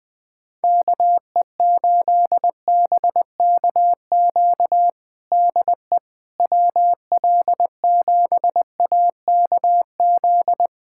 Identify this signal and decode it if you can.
Morse code, heard as a keyed tone, about 21 wpm, 700 Hz. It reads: KE8BKQ DE WL7AKZ
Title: USA Callsign DE Callsign - 6 Characters - 20wpm